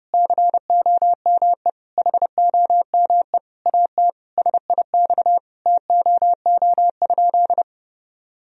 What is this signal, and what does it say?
Morse code, heard as a keyed tone, about 32 wpm, 700 Hz. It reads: COME HOME AT SIX TOO?
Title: Sentences from Top 1000 Words - Review Part 2 - 30wpm